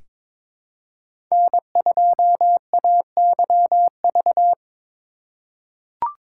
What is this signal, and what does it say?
Morse code, heard as a keyed tone, about 22 wpm, 700 Hz. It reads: N2AYV
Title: USA Call Signs - 4x Character Spacing - 22wpm